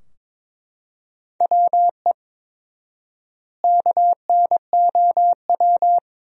Morse code, heard as a keyed tone, about 23 wpm, 700 Hz. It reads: WE KNOW
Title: 4x Word-Spacing - Sentences from Top 500 Words - 22wpm